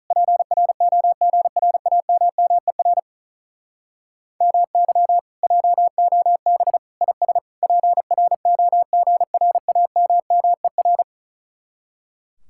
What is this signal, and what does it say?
Morse code, heard as a keyed tone, about 46 wpm, 700 Hz. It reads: PROGRAMMER MY JOB IS PROGRAMMER